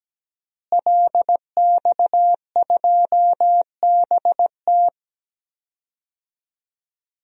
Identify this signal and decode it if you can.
Morse code, heard as a keyed tone, about 17 wpm, 700 Hz. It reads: LX2BT
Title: DXCC-100 Call Signs - 5 Characters - Repeated 2x - 17wpm